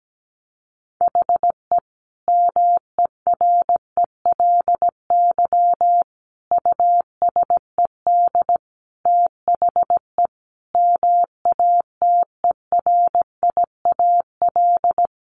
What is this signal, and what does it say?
Morse code, heard as a keyed tone, about 17 wpm, 700 Hz. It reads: HE MERELY USED THE MATERIAL
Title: Sentences from Top 1400 Words 17wpm